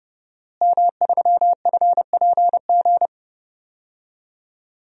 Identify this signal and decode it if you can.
Morse code, heard as a keyed tone, about 32 wpm, 700 Hz. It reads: M3FPG
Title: DXCC-100 Call Signs - 5 Characters - Repeated 3x - 30wpm